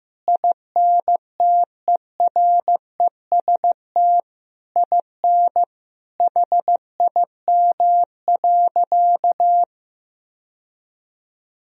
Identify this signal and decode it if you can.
Morse code, heard as a keyed tone, about 15 wpm, 700 Hz. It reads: INTEREST IN HIM.
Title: Sentences from Top 1000 Words - Review Part 3 - 15wpm